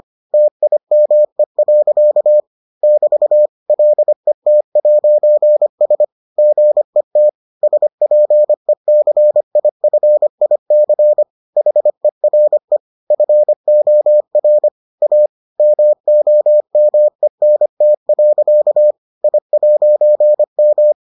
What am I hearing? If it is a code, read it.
Morse code, heard as a keyed tone, about 25 wpm, 600 Hz. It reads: TIME. <BT> LET'S GET SPECIFIC HERE FOR A MOMENT. I'M